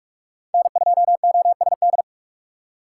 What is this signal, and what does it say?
Morse code, heard as a keyed tone, about 46 wpm, 700 Hz. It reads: N2OSD
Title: USA Call Signs - 5 Characters - 2x Character Spacing - 45wpm